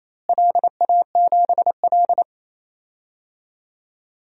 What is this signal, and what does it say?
Morse code, heard as a keyed tone, about 27 wpm, 700 Hz. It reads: LA7L
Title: DXCC-100 Call Signs - 4 Characters - Repeated 3x - 28wpm